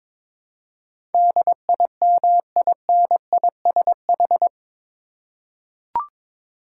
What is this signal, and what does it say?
Morse code, heard as a keyed tone, about 22 wpm, 700 Hz. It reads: DIMINISH
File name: practice / 22wpm / Eight Letter Words - 22wpm